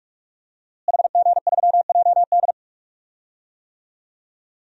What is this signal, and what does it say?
Morse code, heard as a keyed tone, about 47 wpm, 700 Hz. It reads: HG3JD
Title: DXCC-100 Call Signs - 5 Characters - Repeated 3x - 45wpm